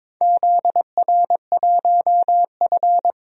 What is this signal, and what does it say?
Morse code, heard as a keyed tone, about 22 wpm, 700 Hz. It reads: ZR1F
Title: DXCC-100 Call Signs - 4 Characters - Repeated 2x - 22wpm